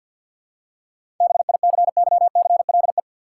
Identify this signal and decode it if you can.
Morse code, heard as a keyed tone, about 52 wpm, 700 Hz. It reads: BICYCLE
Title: Seven Letter Words - 50wpm